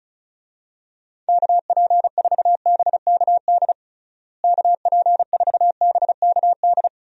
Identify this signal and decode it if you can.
Morse code, heard as a keyed tone, about 36 wpm, 700 Hz. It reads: KP4BKD KP4BKD
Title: DXCC-100 Call Signs - 6 Characters - Repeated 3x - 35wpm